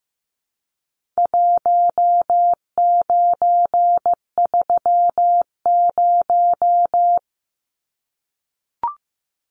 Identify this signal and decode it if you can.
Morse code, heard as a keyed tone, about 15 wpm, 700 Hz. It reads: 1930
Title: Numbers - Contest Style - 4-Digit 15wpm